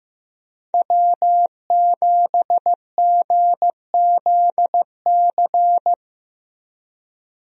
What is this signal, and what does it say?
Morse code, heard as a keyed tone, about 16 wpm, 700 Hz. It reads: W7GZC
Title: USA Call Signs - 5 Characters - Repeated 2x - 15wpm